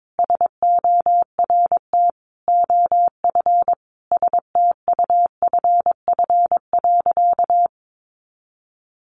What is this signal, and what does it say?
Morse code, heard as a keyed tone, about 22 wpm, 700 Hz. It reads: SORT OF STUFF.
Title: Sentences from Top 1400 Words 22wpm